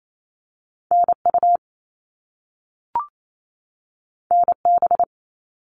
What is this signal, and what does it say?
Morse code, heard as a keyed tone, about 28 wpm, 700 Hz. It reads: NU NB
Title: Canadian Provinces and Territories 28wpm